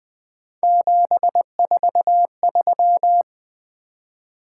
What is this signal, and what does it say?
Morse code, heard as a keyed tone, about 20 wpm, 700 Hz. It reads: 743
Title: Three Letter-Number Combinations - 20wpm